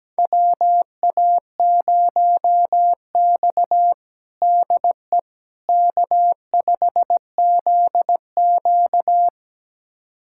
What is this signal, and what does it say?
Morse code, heard as a keyed tone, about 17 wpm, 700 Hz. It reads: WA0X DE K5ZQ
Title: USA Callsign DE Callsign - 4 Characters - 17wpm